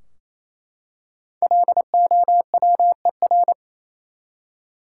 Morse code, heard as a keyed tone, about 28 wpm, 700 Hz. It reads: LOWER
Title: Five Letter Words - 28wpm